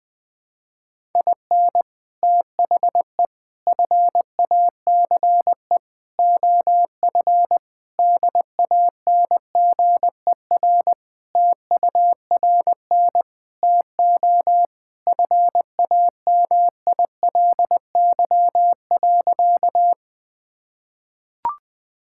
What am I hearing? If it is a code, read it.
Morse code, heard as a keyed tone, about 20 wpm, 700 Hz. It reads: IN THE FACE OF DANGER TURN TO FAMILY.
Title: Sentences with 3-Word Phrases - 20wpm